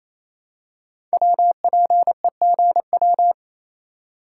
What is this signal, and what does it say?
Morse code, heard as a keyed tone, about 27 wpm, 700 Hz. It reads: WPEGW
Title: Code Groups - 5 Letter:Numbers - 28wpm